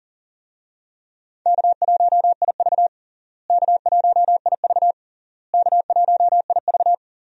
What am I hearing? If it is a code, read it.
Morse code, heard as a keyed tone, about 40 wpm, 700 Hz. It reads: K1IV K1IV K1IV
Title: USA Call Signs - 4 Characters - Repeated 3x - 40wpm